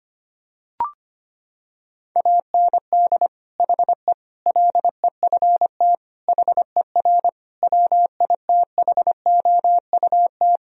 Morse code, heard as a keyed tone, about 25 wpm, 700 Hz. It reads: AND HE LEFT HER WITHOUT
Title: Sentences from Top 1000 Words - Review Part 3 - 25wpm